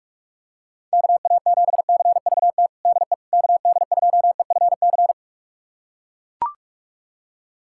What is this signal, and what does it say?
Morse code, heard as a keyed tone, about 41 wpm, 700 Hz. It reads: KA7CVT DE KD2EFC
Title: USA Callsign DE Callsign 45wpm